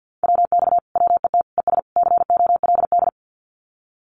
Morse code, heard as a keyed tone, about 50 wpm, 700 Hz. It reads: WX WET ES COLD
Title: Common QSO Phrases Encore 50wpm